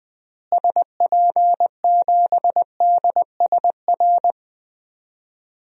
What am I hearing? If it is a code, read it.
Morse code, heard as a keyed tone, about 21 wpm, 700 Hz. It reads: SP7DSR
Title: DXCC-100 Call Signs - 6 Characters - Repeated 3x - 20wpm